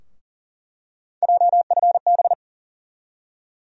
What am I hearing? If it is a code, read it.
Morse code, heard as a keyed tone, about 40 wpm, 700 Hz. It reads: JFB
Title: USA Call Sign Postfixes - 3 Letters 40wpm